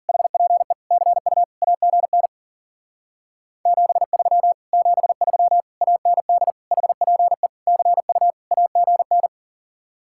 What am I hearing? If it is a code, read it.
Morse code, heard as a keyed tone, about 48 wpm, 700 Hz. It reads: HPE CU AGN 73 73 AND HPE CU AGN
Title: Speed-Racing Common QSO Phrases Encore - 40wpm